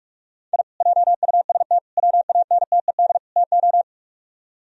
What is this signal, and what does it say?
Morse code, heard as a keyed tone, about 49 wpm, 700 Hz. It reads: I JUST WANTED TO